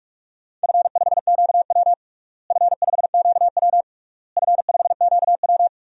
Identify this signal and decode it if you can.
Morse code, heard as a keyed tone, about 42 wpm, 700 Hz. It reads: F5QW F5QW F5QW
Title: DXCC-100 Call Signs - 4 Characters - Repeated 3x - 45wpm